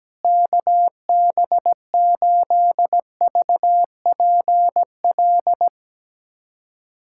Morse code, heard as a keyed tone, about 17 wpm, 700 Hz. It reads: KB8VPL
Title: USA Callsign DE Callsign - 6 Characters - 17wpm